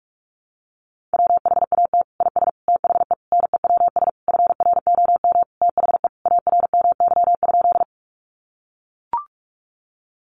Morse code, heard as a keyed tone, about 44 wpm, 700 Hz. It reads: WHAT IS THE NEWS FROM THE ARMY?
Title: Sentences from Top 800 Words 45wpm